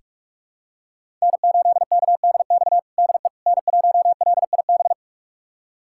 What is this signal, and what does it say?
Morse code, heard as a keyed tone, about 44 wpm, 700 Hz. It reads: N8KDX DE N1LIB
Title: USA Callsign DE Callsign - 5 Characters - 45wpm